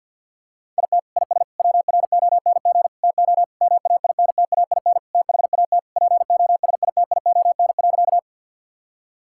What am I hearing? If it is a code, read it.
Morse code, heard as a keyed tone, about 50 wpm, 700 Hz. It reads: IT IS WRONG TO MAINTAIN THAT POSITION.
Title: Sentences from Top 1000 Words - Review Part 5 - 50wpm